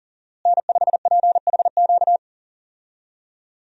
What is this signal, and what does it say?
Morse code, heard as a keyed tone, about 38 wpm, 700 Hz. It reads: N5PHQ
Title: USA Call Signs - 5 Characters - Repeated 2x - 40wpm